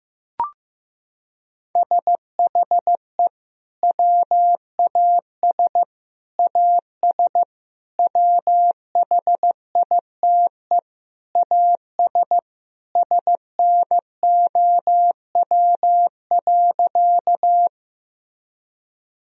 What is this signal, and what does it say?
Morse code, heard as a keyed tone, about 15 wpm, 700 Hz. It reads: SHE WAS AS WHITE AS SNOW.
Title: Sentences from Top 1000 Words - Review Part 3 - 15wpm